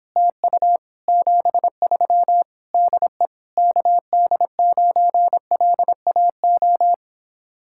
Morse code, heard as a keyed tone, about 29 wpm, 700 Hz. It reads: TU 73 DE KD9LAO